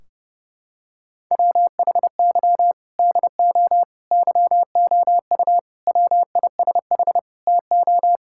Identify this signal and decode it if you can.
Morse code, heard as a keyed tone, about 32 wpm, 700 Hz. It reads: WHY DO YOU WISH TO
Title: Sentences from Top 1600 Words 30wpm